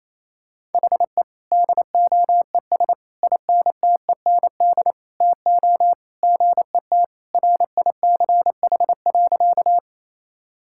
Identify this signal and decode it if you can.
Morse code, heard as a keyed tone, about 28 wpm, 700 Hz. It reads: HE DOES INTEND TO GET RICH.
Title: Sentences from Top 1600 Words 28wpm